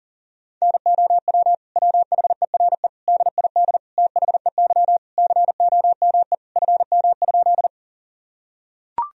Morse code, heard as a keyed tone, about 40 wpm, 700 Hz. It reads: NOW WHERE DID THEY COME FM?
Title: Sentences from Top 200 Words and Abbreviations 40wpm